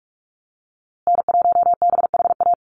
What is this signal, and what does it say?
Morse code, heard as a keyed tone, about 44 wpm, 700 Hz. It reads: N1BHA
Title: USA Call Signs Repeated 3x 45wpm